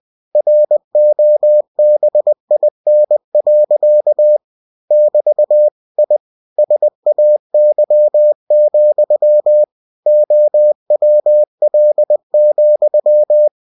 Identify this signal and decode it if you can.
Morse code, heard as a keyed tone, about 20 wpm, 600 Hz. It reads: ROBIN. <BT> I SAY, OWL,
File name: cw-books / winnie-the-pooh / winnie-20wpm-page-0096